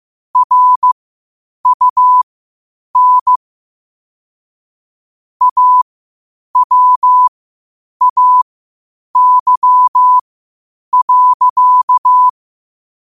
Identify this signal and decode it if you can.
Morse code, heard as a keyed tone, about 15 wpm, spaced as for 8 wpm, 1000 Hz. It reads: RUN AWAY.
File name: qotdfiles / 8wpm